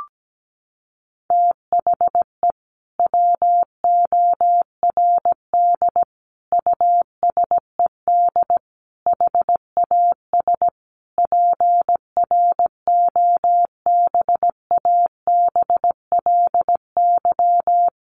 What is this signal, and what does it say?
Morse code, heard as a keyed tone, about 17 wpm, 700 Hz. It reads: THE WORD USED HAS PROBABLY
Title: Sentences from Top 1100 Words 17wpm